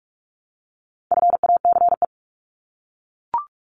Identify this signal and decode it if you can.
Morse code, heard as a keyed tone, about 44 wpm, 700 Hz. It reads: FACE
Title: Top 300 Words 45wpm